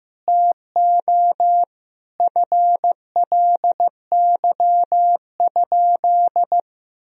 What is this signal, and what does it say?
Morse code, heard as a keyed tone, about 16 wpm, 700 Hz. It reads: TO FLY?
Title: Sentences from Top 1000 Words - Review Part 2 - 15wpm